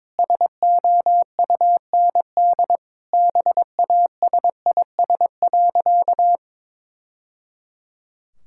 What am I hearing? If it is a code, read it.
Morse code, heard as a keyed tone, about 22 wpm, 700 Hz. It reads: SOUND BASIS.
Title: Sentences from Top 1000 Words - Review Part 5 - 22wpm